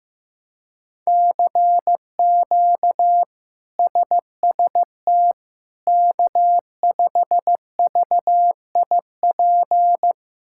Morse code, heard as a keyed tone, about 15 wpm, 700 Hz. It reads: CQ SST K5VIP
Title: K1USN SST QSO Practice - 15wpm